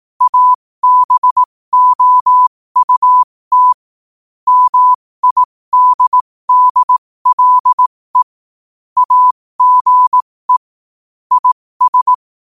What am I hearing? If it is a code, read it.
Morse code, heard as a keyed tone, about 18 wpm, spaced as for 15 wpm, 1000 Hz. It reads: ABOUT MIDDLE AGE IS
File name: qotdfiles / 15wpm